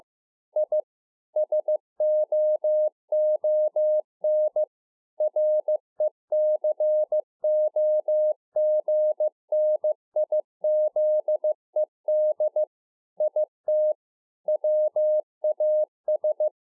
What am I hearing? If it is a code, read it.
Morse code, heard as a keyed tone, about 15 wpm, 610 Hz. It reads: I SOON RECOGNIZED IT WAS